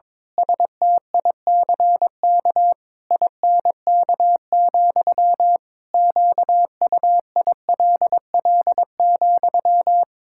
Morse code, heard as a keyed tone, about 22 wpm, 700 Hz. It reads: STICK INK, QUILL,
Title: Pangram Challenge v2 - 22wpm